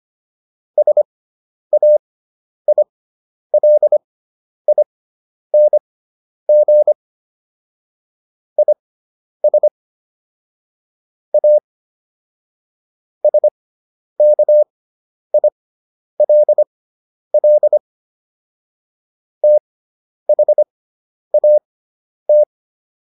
Morse code, heard as a keyed tone, about 25 wpm, spaced as for 10 wpm, 600 Hz. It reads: SAILING IS A SKILL THAT